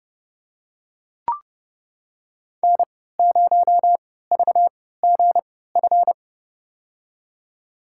Morse code, heard as a keyed tone, about 32 wpm, spaced as for 17 wpm, 700 Hz. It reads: N0VGF
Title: USA Call Signs - 5 Characters - 3x Character Spacing - 30wpm